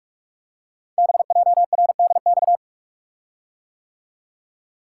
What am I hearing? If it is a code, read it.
Morse code, heard as a keyed tone, about 44 wpm, 700 Hz. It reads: BJRDX
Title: Code Groups - 5 Letters - 45wpm